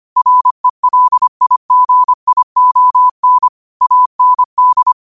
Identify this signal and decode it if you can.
Morse code, heard as a keyed tone, about 25 wpm, 1000 Hz. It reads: RELIGION AND